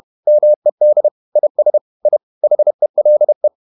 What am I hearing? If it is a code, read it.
Morse code, heard as a keyed tone, about 31 wpm, 600 Hz. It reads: MED IS I HELE
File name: Ferie QRQ 31wpm